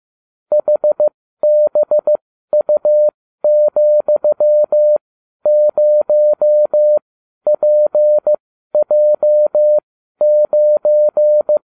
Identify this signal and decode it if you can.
Morse code, heard as a keyed tone, about 15 wpm, spaced as for 12 wpm, 600 Hz. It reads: HBU,0PJ9
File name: cw / ubacw32 2